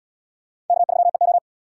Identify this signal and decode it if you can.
Morse code, heard as a keyed tone, about 74 wpm, 700 Hz. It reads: K3EO